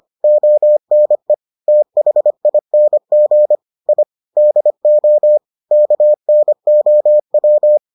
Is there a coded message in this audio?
Yes: Morse code from 0.2 to 7.8 s, heard as a keyed tone, about 25 wpm, 600 Hz. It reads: ONE THING I DO KNOW